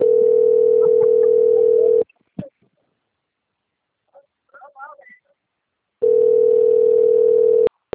Ringback tone (North America), 440 + 480 Hz.